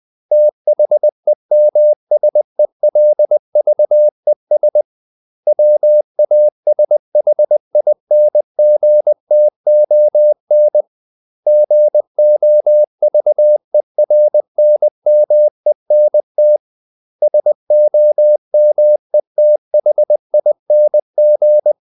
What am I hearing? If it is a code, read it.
Morse code, heard as a keyed tone, about 20 wpm, 600 Hz. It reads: THEMSELVES WASHINGTON GOVERNMENT SOMETHING
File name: CW / Practice / 500MostCommonEnglishWords20